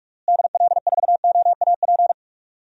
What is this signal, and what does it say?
Morse code, heard as a keyed tone, about 44 wpm, 700 Hz. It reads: DL4OAP
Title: USA Call Signs - 6 Characters - Repeated 3x - 45wpm